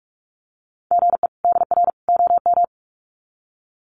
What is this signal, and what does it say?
Morse code, heard as a keyed tone, about 44 wpm, 700 Hz. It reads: GE DR OM
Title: Common QSO Phrases 45wpm